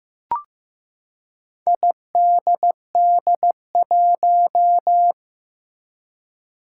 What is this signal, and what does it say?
Morse code, heard as a keyed tone, about 15 wpm, 700 Hz. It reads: IDD1
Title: Four Challenging Letter-Number Combinations 15wpm